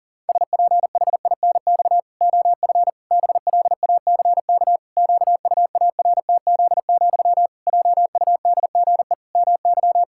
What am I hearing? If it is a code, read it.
Morse code, heard as a keyed tone, about 40 wpm, 700 Hz. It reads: SPHINX OF BLACK QUARTZ, JUDGE MY
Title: Pangram Challenge v2 - 40wpm